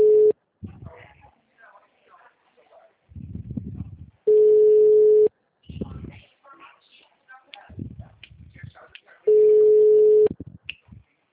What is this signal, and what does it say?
Ringback tone (Europe, 425 Hz), with speech.
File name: out- -2009- - - .1042